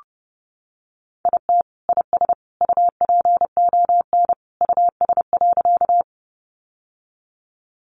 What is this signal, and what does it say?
Morse code, heard as a keyed tone, about 30 wpm, 700 Hz. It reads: IT IS UPON US.
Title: Sentences from Top 1100 Words 30wpm